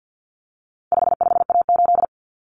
Morse code, heard as a keyed tone, about 50 wpm, 700 Hz. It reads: 55A8
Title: Common Cut Numbers - 4-Digit 50wpm